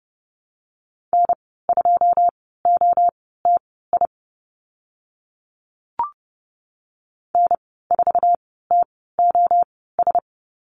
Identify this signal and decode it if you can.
Morse code, heard as a keyed tone, about 31 wpm, spaced as for 17 wpm, 690 Hz. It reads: N2OTI N4TOS
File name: Lesson 15g - Number 5 - Call Signs 30wpm